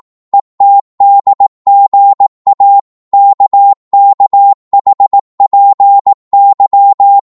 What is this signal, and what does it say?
Morse code, heard as a keyed tone, about 18 wpm, 800 Hz. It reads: ETDGA KKHPY
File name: Morse / ltr-18wpm